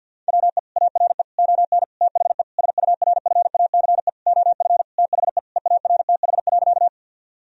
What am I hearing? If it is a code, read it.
Morse code, heard as a keyed tone, about 50 wpm, 700 Hz. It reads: WE ARE ON THE SURFACE OF THE EARTH.
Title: Sentences from Top 1000 Words - Review Part 3 - 50wpm